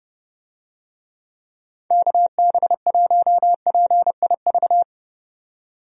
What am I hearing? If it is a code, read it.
Morse code, heard as a keyed tone, about 30 wpm, 700 Hz. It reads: KB1PIV
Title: USA Call Signs - 6 Characters - 30wpm